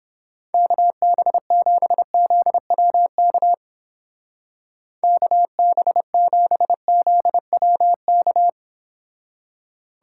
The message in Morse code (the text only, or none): KB7ZWK KB7ZWK